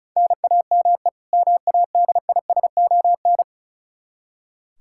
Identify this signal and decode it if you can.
Morse code, heard as a keyed tone, about 34 wpm, 700 Hz. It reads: NAME MADISON
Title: Practice QSO - Exchange 1 - Beginning - 35wpm